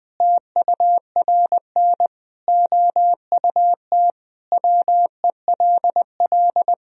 Morse code, heard as a keyed tone, about 21 wpm, 700 Hz. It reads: TURN OUT WELL
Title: Sentences from Top 1000 Words - Review Part 3 - 20wpm